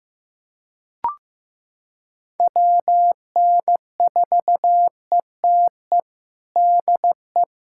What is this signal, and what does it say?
Morse code, heard as a keyed tone, about 15 wpm, 700 Hz. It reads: WN4ETE DE
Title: USA Callsign DE Callsign 15wpm